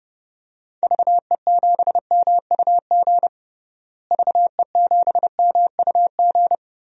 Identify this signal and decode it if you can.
Morse code, heard as a keyed tone, about 32 wpm, 700 Hz. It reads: VE7MUG VE7MUG
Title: DXCC-100 Call Signs - 6 Characters - Repeated 3x - 30wpm